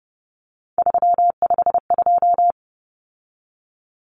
Morse code, heard as a keyed tone, about 30 wpm, 700 Hz. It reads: 352
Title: Numbers - 3-Digit 30wpm